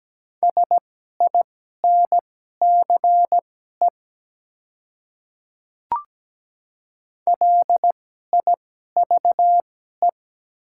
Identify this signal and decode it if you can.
Morse code, heard as a keyed tone, about 17 wpm, spaced as for 12 wpm, 700 Hz. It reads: SINCE LIVE
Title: 2x Character Spacing - Top 300 Words - 17wpm